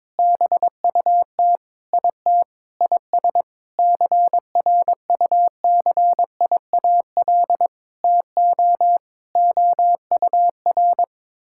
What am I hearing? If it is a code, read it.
Morse code, heard as a keyed tone, about 22 wpm, 700 Hz. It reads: BUT IT IS CRUCIAL TO OUR
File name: Sentences with 9-Word Phrases - 22wpm